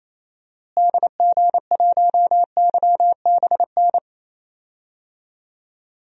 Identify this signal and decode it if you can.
Morse code, heard as a keyed tone, about 28 wpm, 700 Hz. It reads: DG1YBN
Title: DXCC-100 Call Signs - 6 Characters - Repeated 2x - 28wpm